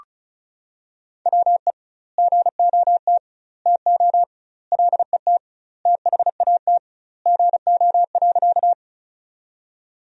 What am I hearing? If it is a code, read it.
Morse code, heard as a keyed tone, about 34 wpm, 700 Hz. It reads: WE GOT TO LET THAT GO.
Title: 2x Word-Spacing - Sentences from Top 300 Words - 35wpm